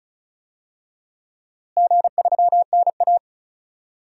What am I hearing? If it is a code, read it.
Morse code, heard as a keyed tone, about 34 wpm, 700 Hz. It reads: G3NA
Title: DXCC-100 Call Signs - 4 Characters - Repeated 2x - 35wpm